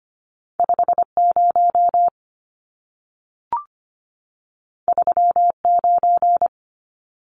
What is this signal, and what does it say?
Morse code, heard as a keyed tone, about 25 wpm, 700 Hz. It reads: 50 39
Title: Numbers - 2-Digit 25wpm